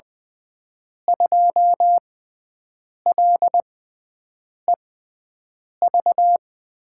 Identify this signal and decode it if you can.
Morse code, heard as a keyed tone, about 20 wpm, spaced as for 7 wpm, 700 Hz. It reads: 2LEV
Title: USA Call Signs - 6 Characters - 6x Character Spacing - 20wpm